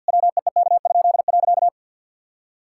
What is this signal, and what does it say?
Morse code, heard as a keyed tone, about 51 wpm, 700 Hz. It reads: WEEK?.
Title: Pangram Challenge v2 - 50wpm